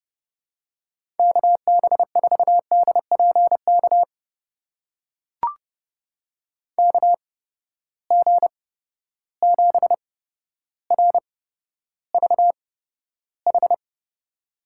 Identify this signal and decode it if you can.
Morse code, heard as a keyed tone, about 32 wpm, spaced as for 13 wpm, 700 Hz. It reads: KB4DPK KG7RVH
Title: USA Call Signs - 6 Characters - 8x Character Spacing - 30wpm